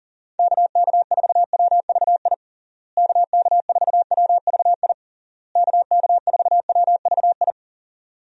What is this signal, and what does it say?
Morse code, heard as a keyed tone, about 40 wpm, 700 Hz. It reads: KK4WVI KK4WVI KK4WVI
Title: USA Call Signs - 6 Characters - Repeated 3x - 40wpm